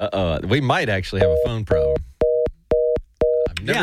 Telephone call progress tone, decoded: reorder tone